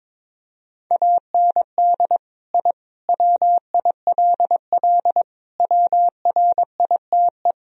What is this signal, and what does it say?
Morse code, heard as a keyed tone, about 22 wpm, 700 Hz. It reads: AND I WILL WRITE
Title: Sentences from Top 1000 Words - Review Part 1 - 22wpm